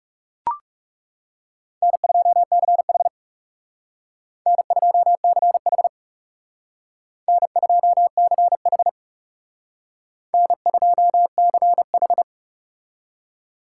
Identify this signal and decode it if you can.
Morse code, heard as a keyed tone, about 38 wpm, 700 Hz. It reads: N2CH N2CH N2CH N2CH